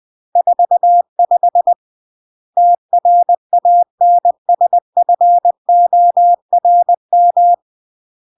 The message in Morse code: 45 TRANSFORM